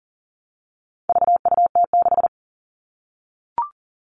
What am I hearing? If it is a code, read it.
Morse code, heard as a keyed tone, about 40 wpm, 700 Hz. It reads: VUT6
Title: All Cut Numbers - 4-Digit 40wpm